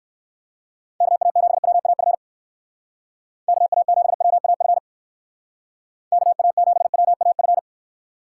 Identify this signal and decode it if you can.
Morse code, heard as a keyed tone, about 58 wpm, 700 Hz. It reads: KA7WAF KA7WAF KA7WAF